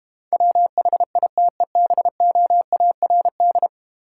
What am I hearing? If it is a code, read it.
Morse code, heard as a keyed tone, about 30 wpm, 700 Hz. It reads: WHITEBOARD
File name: Easy Words - Speed-Racing with Hints - 25wpm